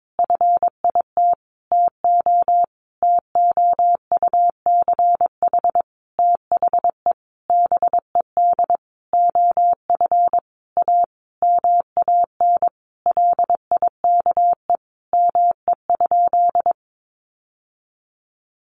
Morse code, heard as a keyed tone, about 22 wpm, 700 Hz. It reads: FIT TO TOUCH THE BED OF A MAN LIKE ME?
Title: Sentences from Top 1200 Words 22wpm